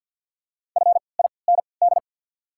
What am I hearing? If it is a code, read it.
Morse code, heard as a keyed tone, about 48 wpm, spaced as for 28 wpm, 700 Hz. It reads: FIND
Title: 3x Character Spacing - Top 100 Words - 50wpm